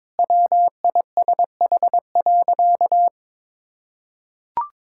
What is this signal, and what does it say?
Morse code, heard as a keyed tone, about 22 wpm, 700 Hz. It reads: WISH.
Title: Sentences from Top 1600 Words 22wpm